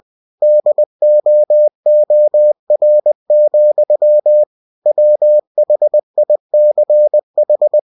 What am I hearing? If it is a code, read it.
Morse code, heard as a keyed tone, about 20 wpm, 600 Hz. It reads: DOOR, WHICH
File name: cw-books / winnie-the-pooh / winnie-20wpm-page-0052